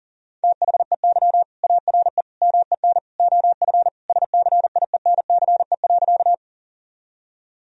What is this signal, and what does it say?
Morse code, heard as a keyed tone, about 37 wpm, 700 Hz. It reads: THEY ARE MEN OF SCIENCE.
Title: Sentences from Top 1000 Words - Review Part 2 - 40wpm